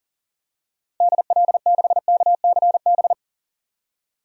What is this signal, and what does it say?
Morse code, heard as a keyed tone, about 40 wpm, 700 Hz. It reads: DL6KCB
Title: DXCC-100 Call Signs - 6 Characters - Repeated 3x - 40wpm